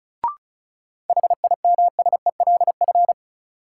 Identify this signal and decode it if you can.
Morse code, heard as a keyed tone, about 36 wpm, 700 Hz. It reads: HIMSELF
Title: Sentences from Top 1500 Words - New Word Drill Alt - 35wpm